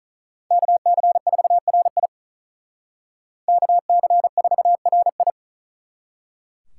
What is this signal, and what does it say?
Morse code, heard as a keyed tone, about 43 wpm, 700 Hz. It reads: KC4RI KC4RI
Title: Speed Racing - USA 2x2 Call Signs 35wpm